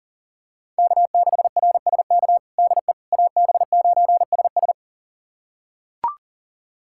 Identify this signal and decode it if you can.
Morse code, heard as a keyed tone, about 40 wpm, 700 Hz. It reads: K6RSK DE AB9SS
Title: USA Callsign DE Callsign - 5 Characters - 40wpm